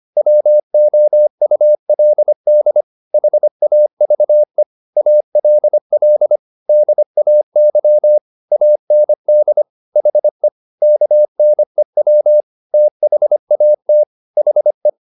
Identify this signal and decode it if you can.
Morse code, heard as a keyed tone, about 25 wpm, 600 Hz. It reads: WOULD HAVE ALL DAY AND HE KNEW THAT HE
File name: the old man and the sea part 1 25wpm